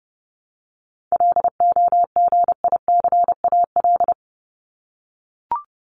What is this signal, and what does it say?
Morse code, heard as a keyed tone, about 29 wpm, 700 Hz. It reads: LOGICAL
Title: Top 1501-1750 Words 30wpm